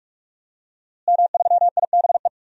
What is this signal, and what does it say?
Morse code, heard as a keyed tone, about 46 wpm, 700 Hz. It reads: M3IBE
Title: DXCC-100 Call Signs - 5 Characters - Repeated 3x - 45wpm